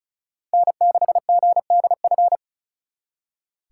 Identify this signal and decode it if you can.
Morse code, heard as a keyed tone, about 45 wpm, 700 Hz. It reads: N6GDF